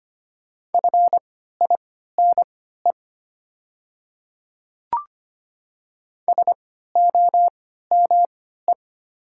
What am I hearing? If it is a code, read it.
Morse code, heard as a keyed tone, about 25 wpm, spaced as for 14 wpm, 700 Hz. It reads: FINE SOME
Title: 3x Character Spacing - Top 500 Words - 25wpm